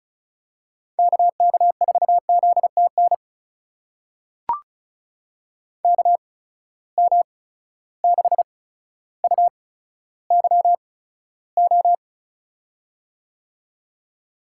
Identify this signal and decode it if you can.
Morse code, heard as a keyed tone, about 36 wpm, spaced as for 15 wpm, 700 Hz. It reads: KK4ZTN KM6UYO
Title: USA Call Signs - 6 Characters - 8x Character Spacing - 35wpm